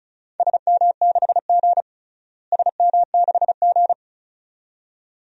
Morse code, heard as a keyed tone, about 36 wpm, 700 Hz. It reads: SM6G SM6G
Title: DXCC-100 Call Signs - 4 Characters - Repeated 2x - 35wpm